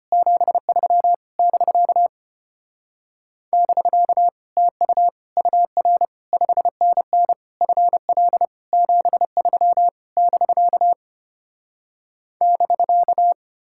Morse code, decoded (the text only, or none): 73 <BK> <BK> TU UR 5NN FL 73 <BK> <BK>